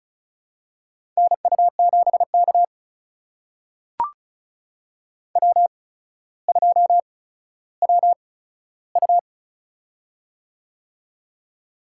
Morse code, heard as a keyed tone, about 36 wpm, spaced as for 15 wpm, 700 Hz. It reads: NU7K W2WU
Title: USA Call Signs - 4 Characters - 8x Character Spacing - 35wpm